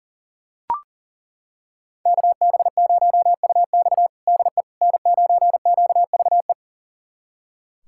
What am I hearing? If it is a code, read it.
Morse code, heard as a keyed tone, about 40 wpm, 700 Hz. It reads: KB0UX DE N9QVE
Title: USA Callsign DE Callsign 40wpm